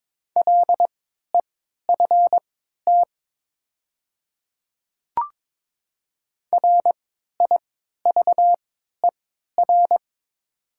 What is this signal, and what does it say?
Morse code, heard as a keyed tone, about 22 wpm, spaced as for 12 wpm, 700 Hz. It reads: LEFT RIVER
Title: 3x Character Spacing - Top 500 Words - 22wpm